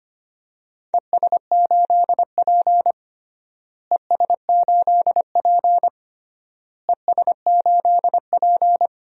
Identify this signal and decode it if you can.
Morse code, heard as a keyed tone, about 25 wpm, 700 Hz. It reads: ES8P ES8P ES8P
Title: DXCC-100 Call Signs - 4 Characters - Repeated 3x - 25wpm